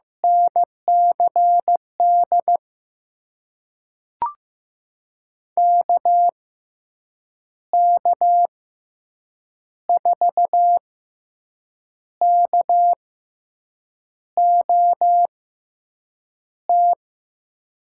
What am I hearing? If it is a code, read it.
Morse code, heard as a keyed tone, about 15 wpm, spaced as for 6 wpm, 700 Hz. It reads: NCD KK4KOT
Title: USA Call Signs - 6 Characters - 6x Character Spacing - 15wpm